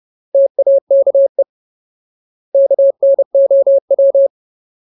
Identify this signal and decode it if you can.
Morse code, heard as a keyed tone, about 30 wpm, 550 Hz. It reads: TAKE KNOW
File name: Ultimate Word List / 30WPM Ultimate